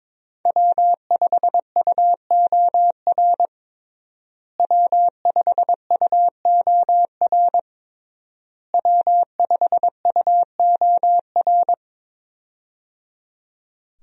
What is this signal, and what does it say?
Morse code, heard as a keyed tone, about 21 wpm, 700 Hz. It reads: W5UOR W5UOR W5UOR
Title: USA Call Signs - 5 Characters - Repeated 3x - 22wpm